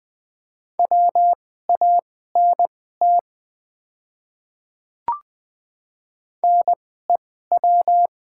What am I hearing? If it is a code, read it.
Morse code, heard as a keyed tone, about 20 wpm, spaced as for 14 wpm, 700 Hz. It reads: WANT NEW
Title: 2x Character Spacing - Top 200 Words - 20wpm